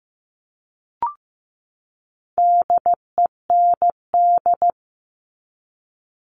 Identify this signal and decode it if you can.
Morse code, heard as a keyed tone, about 15 wpm, 700 Hz. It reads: DEND
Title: All Cut Numbers - 4-Digit 15wpm